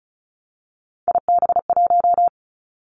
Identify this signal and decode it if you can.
Morse code, heard as a keyed tone, about 36 wpm, 700 Hz. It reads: IB1
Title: Three Challenging Letter-Number Combinations 35wpm